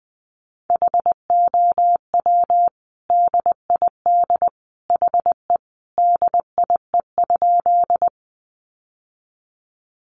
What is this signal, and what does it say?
Morse code, heard as a keyed tone, about 20 wpm, 700 Hz. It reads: HOW DID HE DIE?
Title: Sentences from Top 1300 Words 20wpm